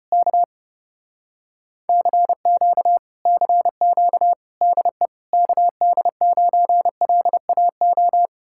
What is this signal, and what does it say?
Morse code, heard as a keyed tone, about 33 wpm, 700 Hz. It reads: K CQ CQ DE KD9LAO